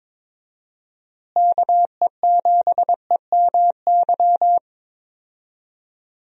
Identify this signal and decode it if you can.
Morse code, heard as a keyed tone, about 22 wpm, 700 Hz. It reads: KE7EMY
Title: USA Call Signs - 6 Characters - Repeated 3x - 22wpm